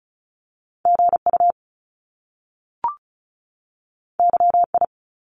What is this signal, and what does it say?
Morse code, heard as a keyed tone, about 36 wpm, 700 Hz. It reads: GU YI
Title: Top Two Letter Combinations in Words 35wpm